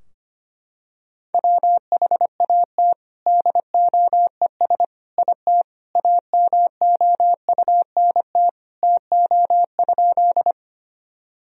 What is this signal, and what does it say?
Morse code, heard as a keyed tone, about 25 wpm, 700 Hz. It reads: WHAT DOES IT AMOUNT TO?
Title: Sentences from Top 1000 Words - Review Part 4 - 25wpm